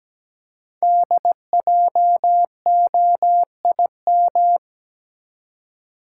Morse code, heard as a keyed tone, about 17 wpm, 700 Hz. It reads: DJOIM
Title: Code Groups - 5 Letters - 17wpm